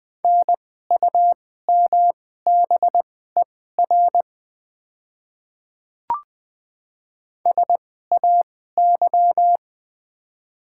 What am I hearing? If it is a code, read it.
Morse code, heard as a keyed tone, about 20 wpm, spaced as for 14 wpm, 700 Hz. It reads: NUMBER SAY
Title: 2x Character Spacing - Top 200 Words - 20wpm